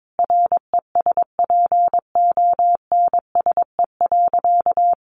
Morse code, heard as a keyed tone, about 22 wpm, 700 Hz. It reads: RESPONSE.
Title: Sentences from Top 1400 Words 22wpm